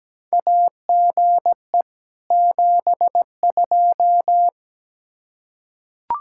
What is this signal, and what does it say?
Morse code, heard as a keyed tone, about 17 wpm, 700 Hz. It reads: AGE 72
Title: Practice QSO - Exchange 3 - More Info - 17wpm